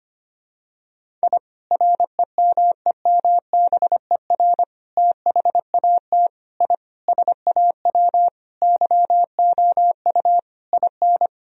Morse code, heard as a keyed tone, about 25 wpm, 700 Hz. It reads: I REMEMBER THAT I SAW YOU IN
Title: Sentences from Top 1000 Words - Review Part 1 - 25wpm